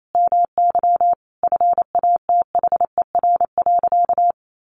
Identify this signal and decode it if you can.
Morse code, heard as a keyed tone, about 28 wpm, 700 Hz. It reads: MY FATHER.
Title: Sentences from Top 1100 Words 28wpm